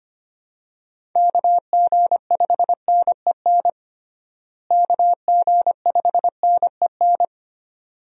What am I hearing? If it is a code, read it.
Morse code, heard as a keyed tone, about 25 wpm, 700 Hz. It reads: KG5NEN KG5NEN
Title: USA Call Signs - 6 Characters - Repeated 2x - 25wpm